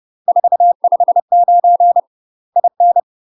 Morse code, heard as a keyed tone, about 30 wpm, 700 Hz. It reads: 459 IN
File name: UK MASTER QSOs TRAINING / 30WPM / UK MASTER QSOs 0024